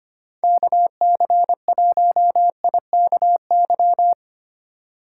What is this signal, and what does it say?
Morse code, heard as a keyed tone, about 25 wpm, 700 Hz. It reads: KC1IKY
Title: USA Call Signs - 6 Characters - 2x Character Spacing - 25wpm